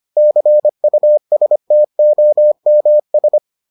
Morse code, heard as a keyed tone, about 25 wpm, 600 Hz. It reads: CUSTOMS